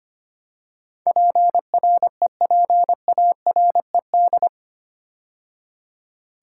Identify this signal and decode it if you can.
Morse code, heard as a keyed tone, about 25 wpm, 700 Hz. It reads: PREPARED
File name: 8 Letter Words - First 3 Letters Given - 25wpm